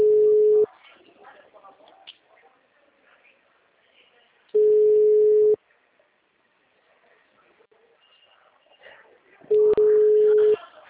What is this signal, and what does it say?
Ringback tone (Europe, 425 Hz).